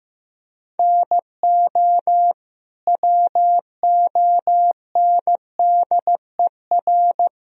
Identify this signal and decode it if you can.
Morse code, heard as a keyed tone, about 15 wpm, 700 Hz. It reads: NO WONDER
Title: Sentences from Top 1000 Words - Review Part 3 - 15wpm